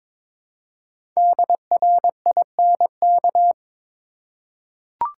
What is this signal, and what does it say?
Morse code, heard as a keyed tone, about 22 wpm, 700 Hz. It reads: DRINK
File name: 5 Letter Words - First 3 Letters Given - 22wpm